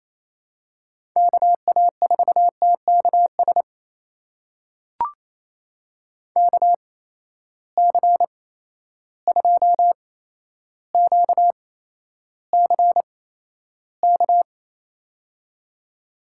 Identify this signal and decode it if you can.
Morse code, heard as a keyed tone, about 28 wpm, spaced as for 12 wpm, 700 Hz. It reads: KA4TKS KC2QCK
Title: USA Call Signs - 6 Characters - 8x Character Spacing - 28wpm